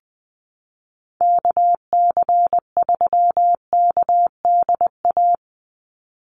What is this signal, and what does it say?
Morse code, heard as a keyed tone, about 20 wpm, 700 Hz. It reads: KC3KDA
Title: USA Call Signs 20wpm